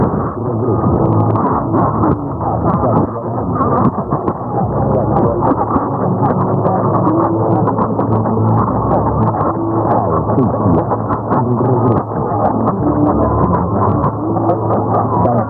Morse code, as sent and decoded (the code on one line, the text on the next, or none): none